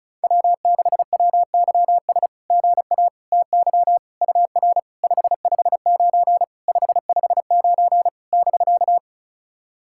Morse code, heard as a keyed tone, about 36 wpm, 690 Hz. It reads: W6WYS GA TY UR 559 559 <BK>
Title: SOTA QSO Practice 35wpm